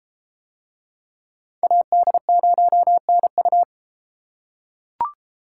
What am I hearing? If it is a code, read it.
Morse code, heard as a keyed tone, about 32 wpm, 700 Hz. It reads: AD0NU